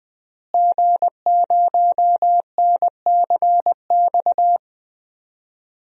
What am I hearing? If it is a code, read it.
Morse code, heard as a keyed tone, about 21 wpm, 700 Hz. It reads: G0NCX
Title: DXCC-100 Call Signs - 5 Characters - Repeated 3x - 20wpm